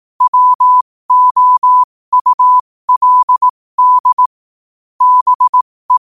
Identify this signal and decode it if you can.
Morse code, heard as a keyed tone, about 18 wpm, spaced as for 15 wpm, 1000 Hz. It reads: WOULD BE
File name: qotdfiles / 15wpm